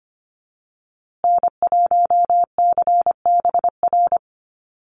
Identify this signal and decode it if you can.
Morse code, heard as a keyed tone, about 25 wpm, 700 Hz. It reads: N1CBR
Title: USA Call Signs Repeated 3x 25wpm